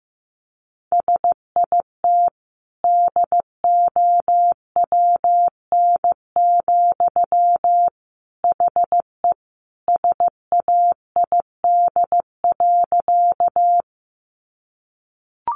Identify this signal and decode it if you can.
Morse code, heard as a keyed tone, about 15 wpm, 700 Hz. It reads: SIT DOWN, HE SAID.
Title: Sentences from Top 500 Words 15wpm